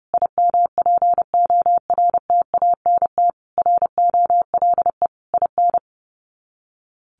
Morse code, heard as a keyed tone, about 28 wpm, 700 Hz. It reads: IMPORTANT ROLE IN
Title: Sets of 5 Words Encore2 30wpm